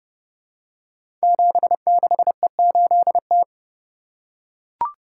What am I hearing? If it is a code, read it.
Morse code, heard as a keyed tone, about 30 wpm, 700 Hz. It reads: 76E8T
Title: Code Groups - 5 Letter:Numbers - 30wpm